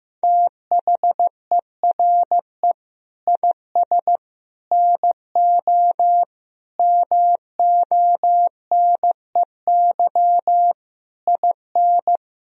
Morse code, heard as a keyed tone, about 15 wpm, 700 Hz. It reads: THERE IS NO MONEY IN
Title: Sentences from Top 1000 Words - Review Part 3 - 15wpm